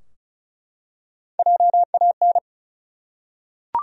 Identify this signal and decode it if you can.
Morse code, heard as a keyed tone, about 34 wpm, 700 Hz. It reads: JAN
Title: USA Common Names - Max 4 Letters - 35wpm